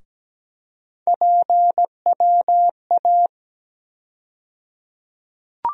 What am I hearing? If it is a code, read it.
Morse code, heard as a keyed tone, about 17 wpm, 700 Hz. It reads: PWA
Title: USA Call Sign Postfixes - 3 Letters 17wpm